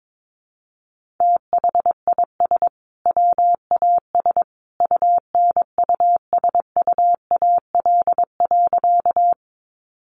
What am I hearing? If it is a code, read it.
Morse code, heard as a keyed tone, about 22 wpm, 700 Hz. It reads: THIS WAS UNUSUAL.
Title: Sentences from Top 1400 Words 22wpm